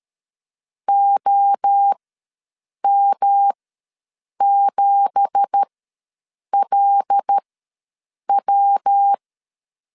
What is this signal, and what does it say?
Morse code, heard as a keyed tone, about 13 wpm, spaced as for 7 wpm, 800 Hz. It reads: OM7LW
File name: LEZIONI / 9 2867 ws 12wpm 3431